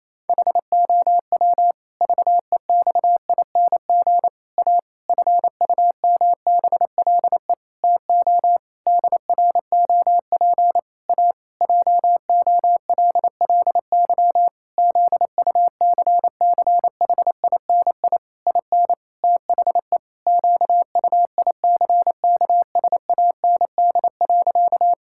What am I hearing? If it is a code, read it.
Morse code, heard as a keyed tone, about 28 wpm, 700 Hz. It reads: HOW VEXING A FUMBLE TO DROP A JOLLY ZUCCHINI IN THE QUICKSAND.
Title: Pangram Challenge v2 - 28wpm